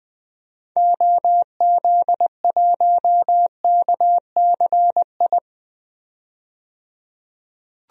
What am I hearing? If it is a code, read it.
Morse code, heard as a keyed tone, about 20 wpm, 700 Hz. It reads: OZ1KCI
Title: DXCC-100 Call Signs - 6 Characters - 20wpm